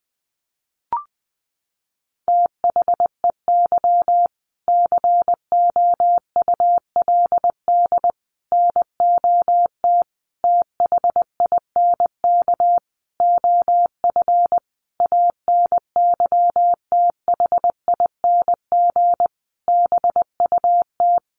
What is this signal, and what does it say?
Morse code, heard as a keyed tone, about 20 wpm, 700 Hz. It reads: THEY COULD NOT THINK OF ANYTHING BUT
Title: Sentences from Top 1000 Words 20wpm